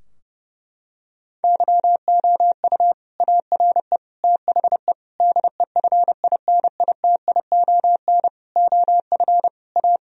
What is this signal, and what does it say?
Morse code, heard as a keyed tone, about 30 wpm, 700 Hz. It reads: YOU ARE THE DEFINITION OF A
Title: Sentences from Top 1000 Words - Review Part 4 - 30wpm